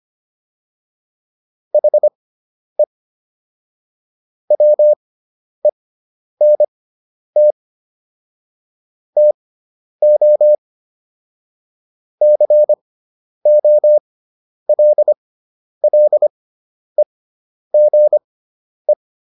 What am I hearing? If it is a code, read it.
Morse code, heard as a keyed tone, about 25 wpm, spaced as for 10 wpm, 600 Hz. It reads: HE WENT TO COLLEGE